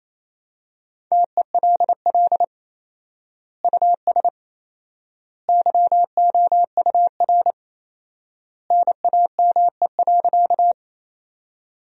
Morse code, heard as a keyed tone, about 28 wpm, 700 Hz. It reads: TELL US YOUR NAME.
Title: 4x Word-Spacing - Sentences from Top 200 Words - 28wpm